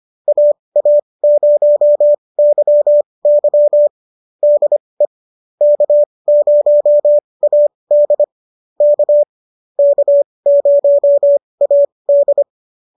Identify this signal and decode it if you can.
Morse code, heard as a keyed tone, about 25 wpm, spaced as for 20 wpm, 590 Hz. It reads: AA0YY DE K0AD K K0AD